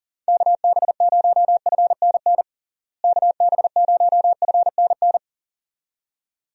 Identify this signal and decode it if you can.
Morse code, heard as a keyed tone, about 40 wpm, 700 Hz. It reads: KB0FNN KB0FNN
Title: USA Call Signs - 6 Characters - Repeated 2x - 40wpm